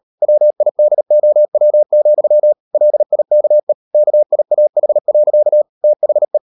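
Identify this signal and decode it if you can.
Morse code, heard as a keyed tone, about 38 wpm, 600 Hz. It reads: WIDOW, LIKE KIAH. THE
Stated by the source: Morse code practice at 38 WPM, sending Ken Follett's novel Never in English, from CW med Gnister.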